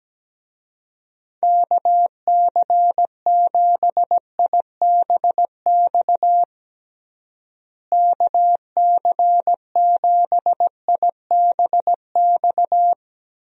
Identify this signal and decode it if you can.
Morse code, heard as a keyed tone, about 17 wpm, 700 Hz. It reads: KC7IBX KC7IBX
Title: USA Call Signs - 6 Characters - Repeated 3x - 17wpm